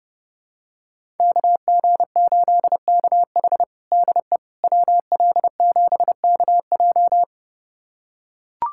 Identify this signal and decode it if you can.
Morse code, heard as a keyed tone, about 30 wpm, 700 Hz. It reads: KG8KH DE WL7KJ
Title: USA Callsign DE Callsign - 5 Characters - 30wpm